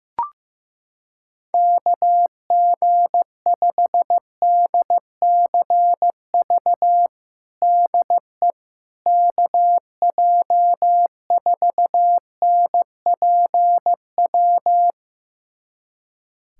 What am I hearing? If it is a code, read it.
Morse code, heard as a keyed tone, about 15 wpm, 700 Hz. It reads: KG5DCV DE KJ4NPW
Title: USA Callsign DE Callsign - 6 Characters - 15wpm